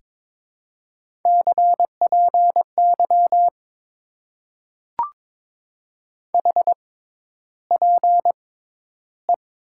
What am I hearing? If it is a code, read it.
Morse code, heard as a keyed tone, about 22 wpm, spaced as for 11 wpm, 700 Hz. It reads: CPY HPE
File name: Top 90 QSO Elements - 6x Character Spacing - 22wpm